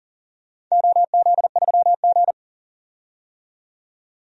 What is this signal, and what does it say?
Morse code, heard as a keyed tone, about 40 wpm, 700 Hz. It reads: OZ3G
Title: DXCC-100 Call Signs - 4 Characters - 40wpm